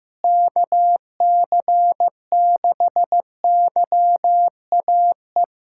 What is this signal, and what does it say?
Morse code, heard as a keyed tone, about 15 wpm, 700 Hz. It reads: KC6YAE
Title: USA Callsign DE Callsign 15wpm